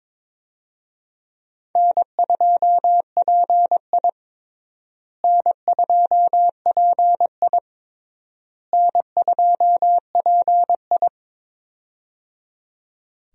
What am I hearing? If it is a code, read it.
Morse code, heard as a keyed tone, about 22 wpm, 700 Hz. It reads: N2PI N2PI N2PI
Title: USA Call Signs - 4 Characters - Repeated 3x - 22wpm